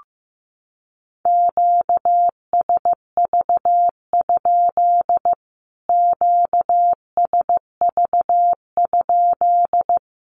Morse code, heard as a keyed tone, about 15 wpm, 700 Hz. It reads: QSV? QSV?
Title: Q-Codes Repeated 3x 15wpm